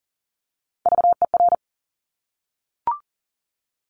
Morse code, heard as a keyed tone, about 40 wpm, 700 Hz. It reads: VER
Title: Top Three Letter Combinations in Words 40wpm